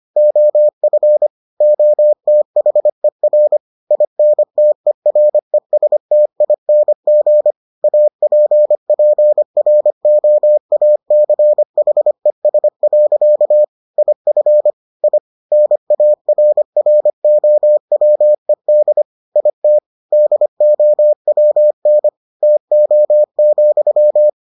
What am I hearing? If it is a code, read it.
Morse code, heard as a keyed tone, about 25 wpm, 600 Hz. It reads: OF OTHER INTERESTING APPROACHES. IF I NARROWED IT DOWN TO,